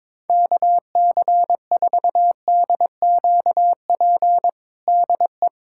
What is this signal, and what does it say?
Morse code, heard as a keyed tone, about 22 wpm, 700 Hz. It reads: KC4DQP DE
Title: USA Callsign DE Callsign 22wpm